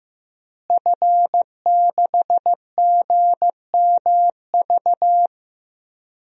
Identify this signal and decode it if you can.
Morse code, heard as a keyed tone, about 15 wpm, 700 Hz. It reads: F6GMV
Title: DXCC-100 Call Signs - 5 Characters - Repeated 3x - 15wpm